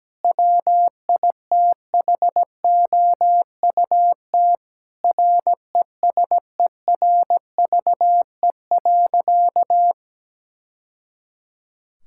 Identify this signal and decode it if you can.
Morse code, heard as a keyed tone, about 17 wpm, 700 Hz. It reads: WITHOUT RESERVE.
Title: Sentences from Top 1000 Words - Review Part 6 - 17wpm